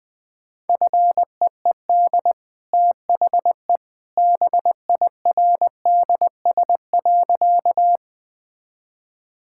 Morse code, heard as a keyed tone, about 20 wpm, 700 Hz. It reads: FEED THE BIRDS.
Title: Easy Sentences with Hints - 20wpm